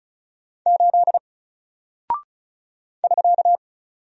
Morse code, heard as a keyed tone, about 34 wpm, 700 Hz. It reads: 8 <SK>
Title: Single Character (Letter:Number:Punctuation:Prosign) Alt 35wpm